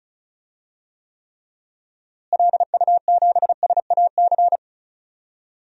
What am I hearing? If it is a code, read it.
Morse code, heard as a keyed tone, about 36 wpm, 700 Hz. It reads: LU7SAC
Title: DXCC-100 Call Signs - 6 Characters - Repeated 3x - 35wpm